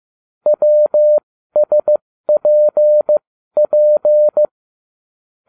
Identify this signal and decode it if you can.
Morse code, heard as a keyed tone, about 15 wpm, spaced as for 12 wpm, 600 Hz. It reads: WSPP